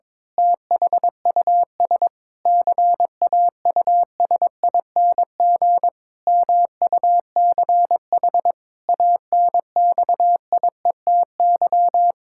Morse code, heard as a keyed tone, about 22 wpm, 700 Hz. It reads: THUS CAUSING MUCH ANXIETY
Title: Pangram Challenge v2 - 22wpm